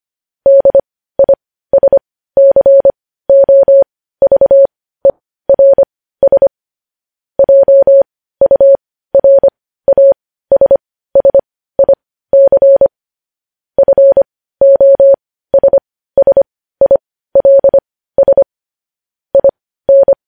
Morse code, heard as a keyed tone, about 25 wpm, spaced as for 15 wpm, 550 Hz. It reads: DISCOVERS JURASSIC FOSSILS IN
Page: news-headlines-in-morse-code-at-15-wpm